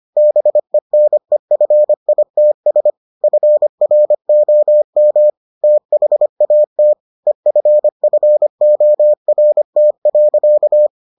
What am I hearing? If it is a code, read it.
Morse code, heard as a keyed tone, about 25 wpm, 600 Hz. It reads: BENEFITS FROM THAT EFFORT.